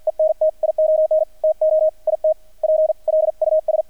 Morse code, heard as a keyed tone, about 48 wpm, 630 Hz, with radio noise. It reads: ENTION TO IT PLUS
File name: archived / CRSnet